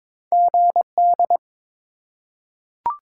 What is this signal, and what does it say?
Morse code, heard as a keyed tone, about 21 wpm, 700 Hz. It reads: GD
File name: Top 90 QSO Elements - 3x Character Spacing - 22wpm